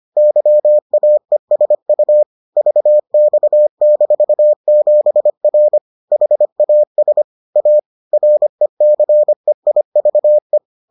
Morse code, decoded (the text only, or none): YAESU VX-7R HAS A RECEIVE